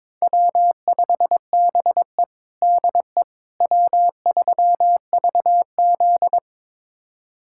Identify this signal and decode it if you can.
Morse code, heard as a keyed tone, about 22 wpm, 700 Hz. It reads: W5BE DE W3VZ
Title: USA Callsign DE Callsign - 4 Characters - 22wpm